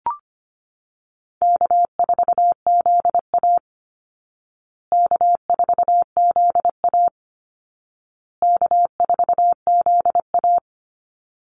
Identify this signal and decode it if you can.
Morse code, heard as a keyed tone, about 25 wpm, 700 Hz. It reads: K4ZA K4ZA K4ZA
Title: Easy Call Signs Repeated 3x - 25wpm